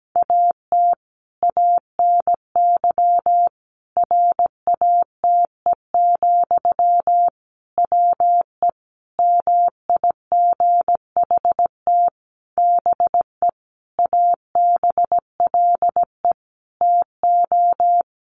Morse code, heard as a keyed tone, about 17 wpm, 700 Hz. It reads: AT ANY RATE, WE MIGHT BE ABLE TO
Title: Sentences from Top 700 Words 17wpm